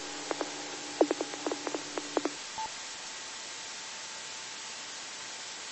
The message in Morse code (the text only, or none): none